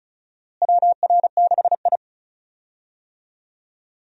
Morse code, heard as a keyed tone, about 34 wpm, 700 Hz. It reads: WR6I
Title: USA Call Signs - 4 Characters - Repeated 2x - 35wpm